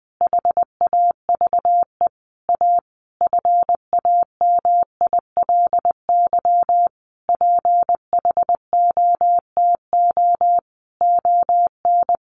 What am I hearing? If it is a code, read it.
Morse code, heard as a keyed tone, about 20 wpm, 700 Hz. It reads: HAVE A FAMILY PHOTO ON
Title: Sentences from Top 1200 Words 20wpm